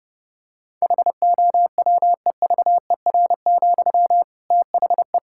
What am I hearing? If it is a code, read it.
Morse code, heard as a keyed tone, about 32 wpm, 700 Hz. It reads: HOWEVER, THE
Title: Sentences from Top 1000 Words - Review Part 6 - 30wpm